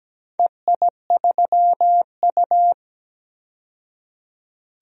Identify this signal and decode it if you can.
Morse code, heard as a keyed tone, about 17 wpm, 700 Hz. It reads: EI3U
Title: DXCC-100 Call Signs - 4 Characters - 17wpm